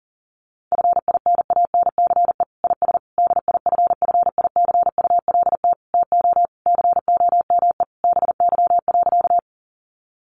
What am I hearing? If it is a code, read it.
Morse code, heard as a keyed tone, about 40 wpm, 700 Hz. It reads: FINANCE IS DIFFICULT TO COME BY.
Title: Sentences from Top 1500 Words 40wpm